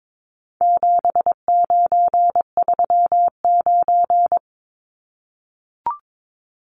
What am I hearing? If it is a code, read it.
Morse code, heard as a keyed tone, about 22 wpm, 700 Hz. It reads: 7939
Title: Numbers - 4-Digit 22wpm